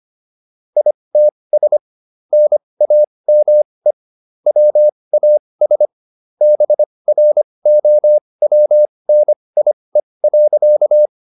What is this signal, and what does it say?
Morse code, heard as a keyed tone, about 25 wpm, spaced as for 20 wpm, 600 Hz. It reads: ITS NAME WAS BROWNIE.